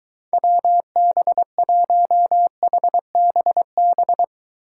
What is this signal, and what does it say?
Morse code, heard as a keyed tone, about 23 wpm, 700 Hz. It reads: WB1HBB